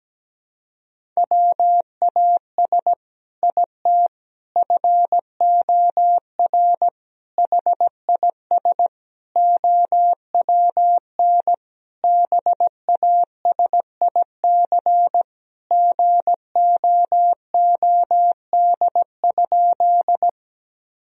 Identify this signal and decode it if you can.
Morse code, heard as a keyed tone, about 17 wpm, 700 Hz. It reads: WAS IT FOR HIS OWN BASIC GOOD?
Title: Sentences from Top 1000 Words - Review Part 3 - 17wpm